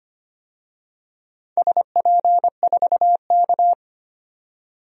Morse code, heard as a keyed tone, about 25 wpm, 700 Hz. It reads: SP4K
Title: DXCC-100 Call Signs - 4 Characters - 25wpm